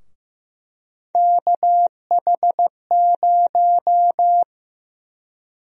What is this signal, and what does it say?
Morse code, heard as a keyed tone, about 15 wpm, 700 Hz. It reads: KH0
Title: USA Call Sign Prefixes - 15wpm